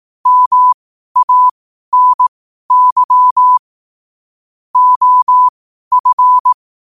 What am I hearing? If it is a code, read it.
Morse code, heard as a keyed tone, about 18 wpm, spaced as for 12 wpm, 1000 Hz. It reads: MANY OF
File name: qotdfiles / 12wpm